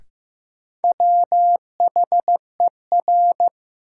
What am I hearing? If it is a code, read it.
Morse code, heard as a keyed tone, about 15 wpm, 700 Hz. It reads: WHER
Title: Sentences from Top 1500 Words - New Word Drill - 15wpm